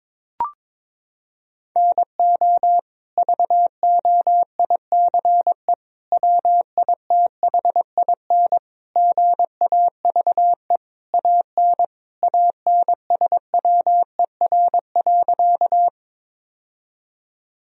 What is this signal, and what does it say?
Morse code, heard as a keyed tone, about 22 wpm, 700 Hz. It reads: NO VOICE WITHIN GAVE AN ANSWER.
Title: Sentences from Top 1000 Words - Review Part 4 - 22wpm